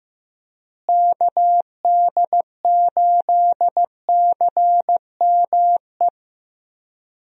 Morse code, heard as a keyed tone, about 15 wpm, 700 Hz. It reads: KD8CME
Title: USA Call Signs - 6 Characters - Repeated 3x - 15wpm